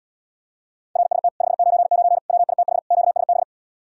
Morse code, heard as a keyed tone, about 77 wpm, 700 Hz. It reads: RST 599 WITH QSB